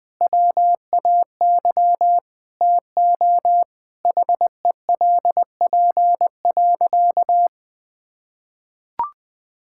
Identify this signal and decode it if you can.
Morse code, heard as a keyed tone, about 20 wpm, 700 Hz. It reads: WAY TO HELP.
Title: Sentences with 6-Word Phrases - Spoken Beginning - 20wpm